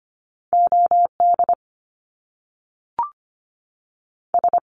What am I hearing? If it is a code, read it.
Morse code, heard as a keyed tone, about 25 wpm, 700 Hz. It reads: O D S
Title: Top Two Letter Combinations in Words 25wpm